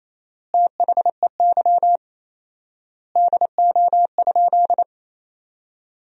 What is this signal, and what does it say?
Morse code, heard as a keyed tone, about 28 wpm, 700 Hz. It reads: THEY DO?
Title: 4x Word-Spacing - Sentences from Top 100 Words - 28wpm